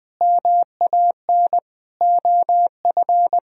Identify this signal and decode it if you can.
Morse code, heard as a keyed tone, about 20 wpm, 700 Hz. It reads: MAN OF
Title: Sentences from Top 1000 Words - Review Part 2 - 20wpm